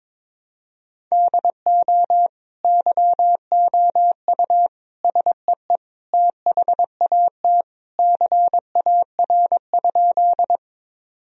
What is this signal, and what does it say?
Morse code, heard as a keyed tone, about 21 wpm, 700 Hz. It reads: DO YOU SEE THAT CAR?
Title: Sentences from Top 1000 Words - Review Part 1 - 22wpm